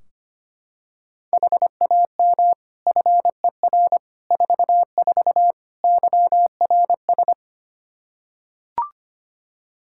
Morse code, heard as a keyed tone, about 25 wpm, 700 Hz. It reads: HAM FER 44 YRS
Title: Practice QSO - Exchange 3 - More Info - 25wpm